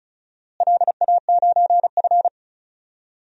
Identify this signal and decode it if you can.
Morse code, heard as a keyed tone, about 36 wpm, 700 Hz. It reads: LA9F
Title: DXCC-100 Call Signs - 4 Characters - Repeated 2x - 35wpm